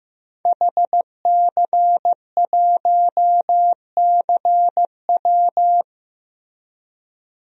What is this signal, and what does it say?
Morse code, heard as a keyed tone, about 16 wpm, 700 Hz. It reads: HC1CW
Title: DXCC-100 Call Signs - 5 Characters - Repeated 3x - 15wpm